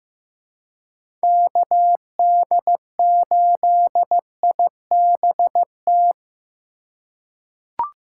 Morse code, heard as a keyed tone, about 15 wpm, 700 Hz. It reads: KD8IBT
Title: USA Call Signs - 6 Characters - 3x Character Spacing - 15wpm